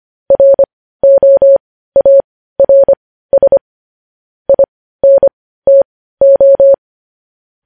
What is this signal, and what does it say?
Morse code, heard as a keyed tone, about 25 wpm, spaced as for 15 wpm, 550 Hz. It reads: ROARS INTO
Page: news-headlines-in-morse-code-at-15-wpm